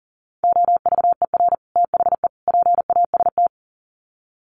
Morse code, heard as a keyed tone, about 40 wpm, 700 Hz. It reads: OVER THE PAST
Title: Sets of 3 Words 40wpm